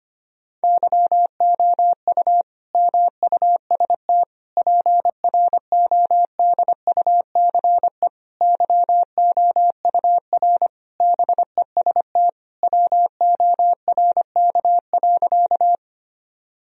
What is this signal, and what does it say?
Morse code, heard as a keyed tone, about 25 wpm, 700 Hz. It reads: YOU MUST PRODUCE YOUR BEST WORK.
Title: Sentences from Top 1000 Words - Review Part 3 - 25wpm